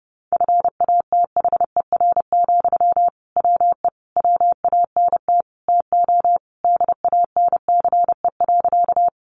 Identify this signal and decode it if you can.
Morse code, heard as a keyed tone, about 30 wpm, 700 Hz. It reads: FATHER, WE WANT TO DANCE.
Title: Sentences from Top 700 Words 30wpm